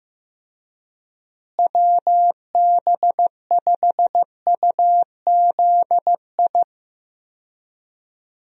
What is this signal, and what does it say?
Morse code, heard as a keyed tone, about 15 wpm, 700 Hz. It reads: WB5UZI
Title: USA Call Signs - 6 Characters - Repeated 3x - 15wpm